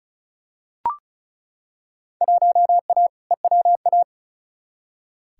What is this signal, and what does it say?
Morse code, heard as a keyed tone, about 36 wpm, 700 Hz. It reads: 1A EWA
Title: Field Day Exchanges Encore - 35wpm